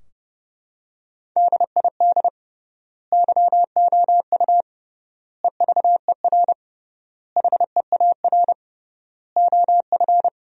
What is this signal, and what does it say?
Morse code, heard as a keyed tone, about 30 wpm, 700 Hz. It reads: DID YOU EVER HEAR OF
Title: 3x Word-Spacing - Sentences from Top 400 Words - 30wpm